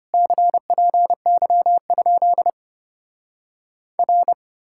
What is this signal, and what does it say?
Morse code, heard as a keyed tone, about 33 wpm, 700 Hz. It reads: CPY? R